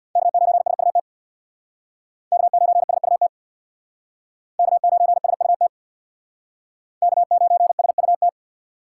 Morse code, heard as a keyed tone, about 63 wpm, 700 Hz. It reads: K9SUT K9SUT K9SUT K9SUT